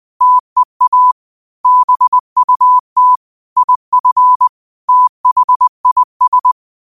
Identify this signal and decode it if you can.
Morse code, heard as a keyed tone, about 20 wpm, 1000 Hz. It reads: TEA BUT IF THIS
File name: qotdfiles / 20wpm